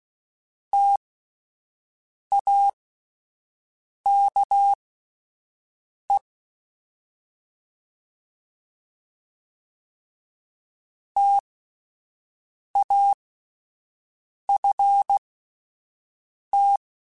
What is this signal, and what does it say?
Morse code, heard as a keyed tone, about 16 wpm, spaced as for 5 wpm, 800 Hz. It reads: TAKE TAFT
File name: Spaced Alphas / Spaced Alphas mp3s / Ts sp @10